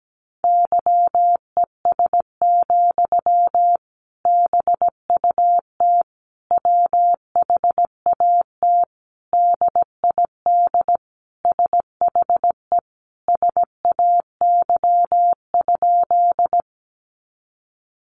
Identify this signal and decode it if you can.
Morse code, heard as a keyed tone, about 17 wpm, 700 Hz. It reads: YES, BUT WHAT DID SHE SAY?
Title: Sentences from Top 500 Words 17wpm